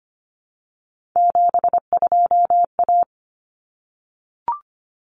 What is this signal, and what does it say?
Morse code, heard as a keyed tone, about 25 wpm, 700 Hz. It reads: 72A
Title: Common Cut Numbers - 3-Digit 25wpm